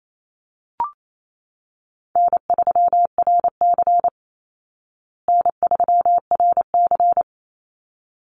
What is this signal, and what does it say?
Morse code, heard as a keyed tone, about 28 wpm, 700 Hz. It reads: N3RC N3RC
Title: Easy Call Signs Repeated 2x - 28wpm